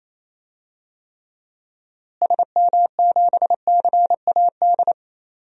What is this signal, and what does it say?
Morse code, heard as a keyed tone, about 28 wpm, 700 Hz. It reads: SM7CAD
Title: DXCC-100 Call Signs - 6 Characters - Repeated 2x - 28wpm